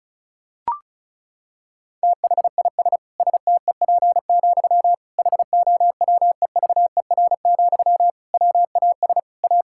Morse code, heard as a keyed tone, about 36 wpm, 700 Hz. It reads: THIS STEP, HOWEVER, WAS A
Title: Sentences from Top 1000 Words - Review Part 6 - 35wpm